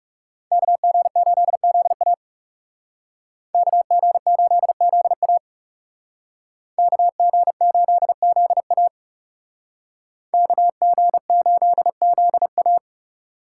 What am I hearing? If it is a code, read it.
Morse code, heard as a keyed tone, about 38 wpm, 700 Hz. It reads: KG8ZA KG8ZA KG8ZA KG8ZA